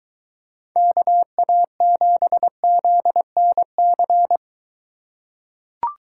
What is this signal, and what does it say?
Morse code, heard as a keyed tone, about 23 wpm, 700 Hz. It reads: KA7ZNC